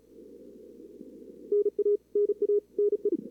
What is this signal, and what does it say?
Morse code, heard as a keyed tone, about 35 wpm, 440 Hz. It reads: NA NA NE
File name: CWSprint / Feb2022 / k5tr-feb-na-sprint-cw-k5ot -0000